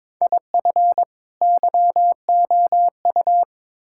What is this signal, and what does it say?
Morse code, heard as a keyed tone, about 21 wpm, 700 Hz. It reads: IF YOU
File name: practice / 22wpm / Build A Phrase - 5-Word Phrases - 22wpm